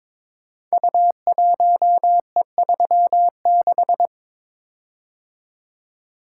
Morse code, heard as a keyed tone, about 22 wpm, 700 Hz. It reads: U1E36
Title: Code Groups - 5 Letter:Numbers - 22wpm